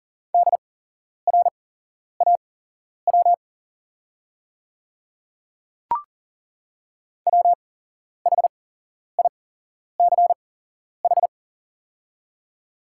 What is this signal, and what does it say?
Morse code, heard as a keyed tone, about 40 wpm, spaced as for 11 wpm, 700 Hz. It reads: DRAW WHICH
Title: 8x Character Spacing - Top 300 Words - 40wpm